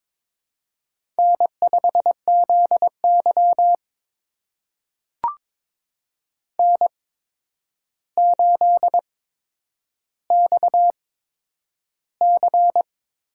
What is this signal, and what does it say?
Morse code, heard as a keyed tone, about 23 wpm, spaced as for 10 wpm, 700 Hz. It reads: N5ZY N8XC
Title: USA Call Signs - 4 Characters - 8x Character Spacing - 22wpm